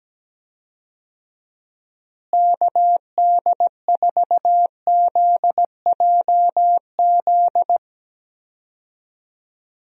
Morse code, heard as a keyed tone, about 17 wpm, 700 Hz. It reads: KD4ZJZ
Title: USA Call Signs - 6 Characters - Repeated 2x - 17wpm